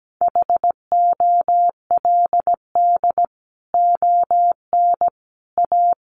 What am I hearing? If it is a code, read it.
Morse code, heard as a keyed tone, about 17 wpm, 700 Hz. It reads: HOLD ON A
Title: Sentences from Top 500 Words 17wpm